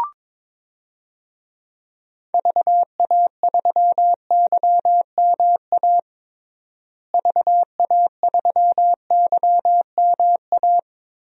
Morse code, heard as a keyed tone, about 22 wpm, 700 Hz. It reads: VA3YMA VA3YMA
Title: DXCC-100 Call Signs - 6 Characters - Repeated 2x - 22wpm